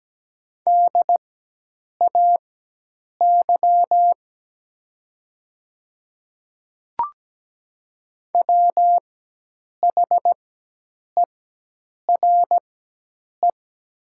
Morse code, heard as a keyed tone, about 17 wpm, spaced as for 8 wpm, 700 Hz. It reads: DAY WHERE
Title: 4x Character Spacing - Top 200 Words - 17wpm